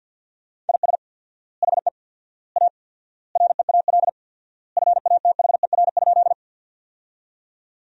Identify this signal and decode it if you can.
Morse code, heard as a keyed tone, about 47 wpm, 700 Hz. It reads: IS HE A REAL FATHER?
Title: 4x Word-Spacing - Sentences from Top 300 Words - 50wpm